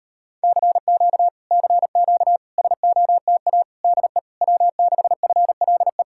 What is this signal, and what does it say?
Morse code, heard as a keyed tone, about 33 wpm, 700 Hz. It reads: CQ CQ SOTA DE W6FLE